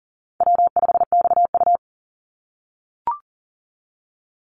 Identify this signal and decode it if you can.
Morse code, heard as a keyed tone, about 40 wpm, 700 Hz. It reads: W5XU
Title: Easy Call Signs - 40wpm